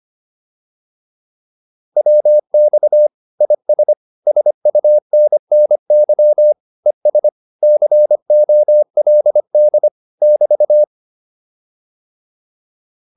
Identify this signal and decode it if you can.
Morse code, heard as a keyed tone, about 25 wpm, 600 Hz. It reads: WX IS SUNNY ES COLD <BT>